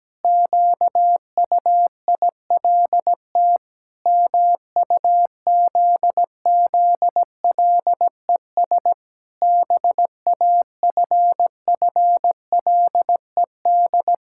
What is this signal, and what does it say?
Morse code, heard as a keyed tone, about 17 wpm, 700 Hz. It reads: QUILT MUZZLES BAFFLED
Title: Pangram Challenge v2 - 17wpm